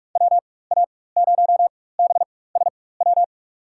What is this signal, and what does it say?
Morse code, heard as a keyed tone, about 45 wpm, spaced as for 21 wpm, 690 Hz. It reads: WA0BSW
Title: USA Call Signs - 4x Character Spacing - 45wpm